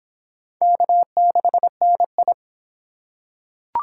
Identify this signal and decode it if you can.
Morse code, heard as a keyed tone, about 23 wpm, 700 Hz. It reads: K6NI